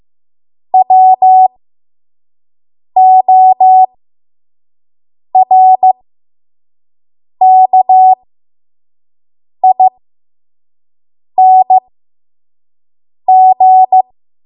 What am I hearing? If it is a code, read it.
Morse code, heard as a keyed tone, about 15 wpm, spaced as for 5 wpm, 750 Hz. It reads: WORKING
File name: Morse / 05WPM